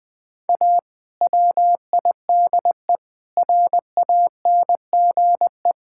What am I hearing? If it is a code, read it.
Morse code, heard as a keyed tone, about 20 wpm, 690 Hz. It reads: A WIDE RANGE
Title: Build A Phrase - 4-Word Phrases - 20wpm